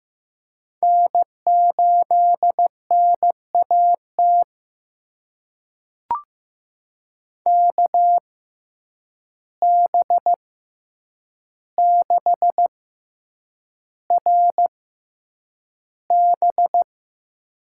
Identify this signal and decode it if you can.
Morse code, heard as a keyed tone, about 15 wpm, spaced as for 8 wpm, 700 Hz. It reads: N8NAT KB6RB
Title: USA Call Signs - 5 Characters - 6x Character Spacing - 15wpm